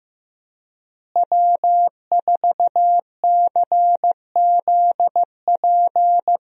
Morse code, heard as a keyed tone, about 15 wpm, 700 Hz. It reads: W4CZP
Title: USA Call Signs - 5 Characters - Repeated 3x - 15wpm